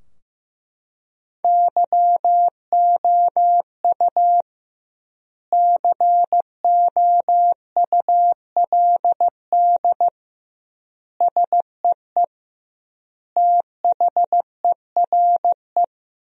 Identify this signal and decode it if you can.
Morse code, heard as a keyed tone, about 15 wpm, 700 Hz. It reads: YOU COULD SEE THERE
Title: 2x Word-Spacing - Sentences from Top 500 Words - 15wpm